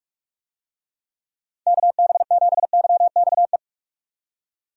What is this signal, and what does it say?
Morse code, heard as a keyed tone, about 46 wpm, 700 Hz. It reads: KB7YXE
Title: USA Call Signs - 6 Characters - 45wpm